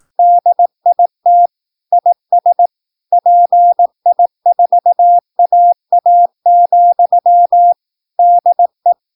Morse code, heard as a keyed tone, about 18 wpm, 700 Hz. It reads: DIT IS PI4AA, DE
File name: PI4AA Uitzending